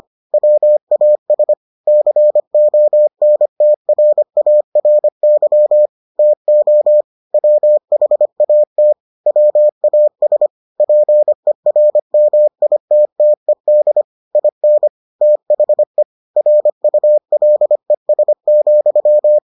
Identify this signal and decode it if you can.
Morse code, heard as a keyed tone, about 25 wpm, 600 Hz. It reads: WAS CONTRARY TO WHAT WAS PERMITTED IN THE RULES,